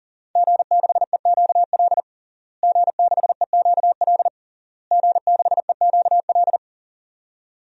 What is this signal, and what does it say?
Morse code, heard as a keyed tone, about 40 wpm, 700 Hz. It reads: G6EQL G6EQL G6EQL
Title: DXCC-100 Call Signs - 5 Characters - Repeated 3x - 40wpm